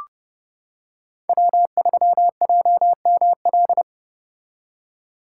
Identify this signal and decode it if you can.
Morse code, heard as a keyed tone, about 30 wpm, 700 Hz. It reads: W3JML